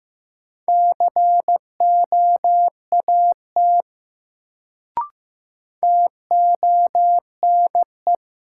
Morse code, heard as a keyed tone, about 15 wpm, 700 Hz. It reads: COAT TONE
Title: Sentences from Top 1600 Words - New Word Drill Alt - 15wpm